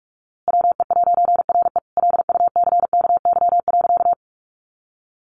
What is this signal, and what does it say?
Morse code, heard as a keyed tone, about 44 wpm, 700 Hz. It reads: WE'RE LUCKY.
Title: A Sound Of Thunder 45wpm